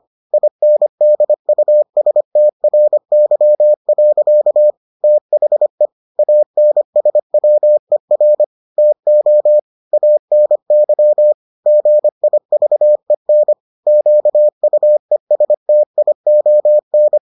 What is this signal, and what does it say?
Morse code, heard as a keyed tone, about 25 wpm, 600 Hz. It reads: INDUSTRY. THE ANSWER TO ANY GIVEN QUESTION